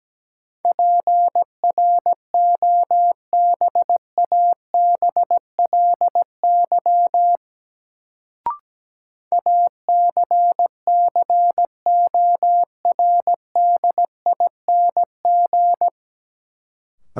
Morse code, heard as a keyed tone, about 17 wpm, 700 Hz. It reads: PROBABLY ACCORDING
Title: Sentences from Top 700 Words - New Word Drill - 17wpm